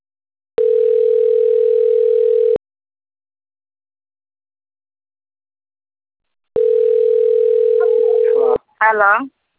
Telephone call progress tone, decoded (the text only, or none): ringback tone